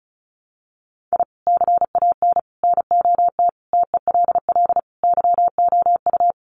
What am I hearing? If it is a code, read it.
Morse code, heard as a keyed tone, about 34 wpm, 700 Hz. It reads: I CAN NOT TELL YOU
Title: Sets of 5 Words 35wpm